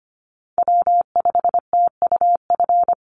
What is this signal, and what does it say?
Morse code, heard as a keyed tone, about 25 wpm, 700 Hz. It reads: W5TUF
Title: Call Signs - 25wpm